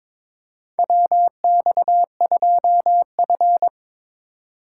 Morse code, heard as a keyed tone, about 22 wpm, 700 Hz. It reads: WX2F
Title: USA Call Signs - 4 Characters - 4x Character Spacing - 22wpm